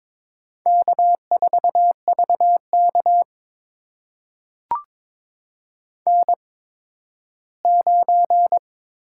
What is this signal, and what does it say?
Morse code, heard as a keyed tone, about 22 wpm, spaced as for 13 wpm, 700 Hz. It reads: K4VK N9
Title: USA Call Signs - 4 Characters - 8x Character Spacing - 22wpm